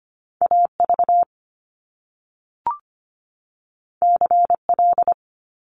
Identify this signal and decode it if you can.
Morse code, heard as a keyed tone, about 25 wpm, 700 Hz. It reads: AV CL
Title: Top Two Letter Combinations in Words 25wpm